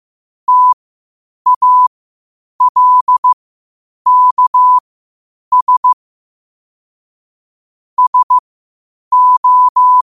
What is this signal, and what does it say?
Morse code, heard as a keyed tone, about 15 wpm, spaced as for 8 wpm, 1000 Hz. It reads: TALKS SO
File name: qotdfiles / 8wpm